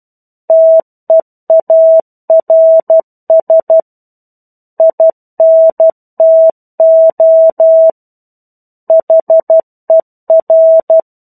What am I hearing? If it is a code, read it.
Morse code, heard as a keyed tone, about 12 wpm, 650 Hz. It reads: TEARS INTO HER
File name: 01-Chapter-01 12 WPM